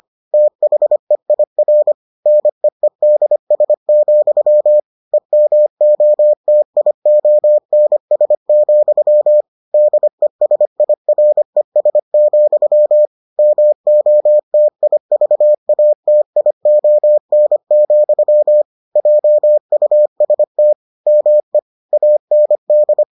Morse code, heard as a keyed tone, about 25 wpm, 600 Hz. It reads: THEIR NEEDS, EMOTIONS, DESIRES, MOTIVATION, JUST ME AND